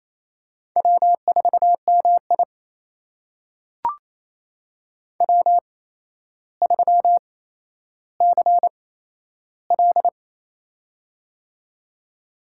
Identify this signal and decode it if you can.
Morse code, heard as a keyed tone, about 27 wpm, spaced as for 12 wpm, 700 Hz. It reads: W4MI W3CL
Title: USA Call Signs - 4 Characters - 8x Character Spacing - 28wpm